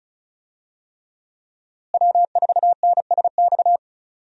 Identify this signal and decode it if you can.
Morse code, heard as a keyed tone, about 36 wpm, 700 Hz. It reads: W4NSX
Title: USA Call Signs - 5 Characters - Repeated 3x - 35wpm